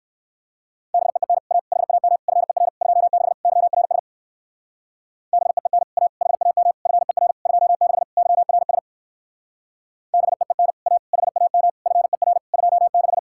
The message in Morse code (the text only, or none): BEEN A HAM FER 26 YRS BEEN A HAM FER 26 YRS BEEN A HAM FER 26